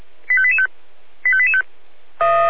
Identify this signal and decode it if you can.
Five-tone selective call (ZVEI), heard as quick runs of five tones, tones of about 70 ms. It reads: 85885 85885
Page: SDR/ZVEI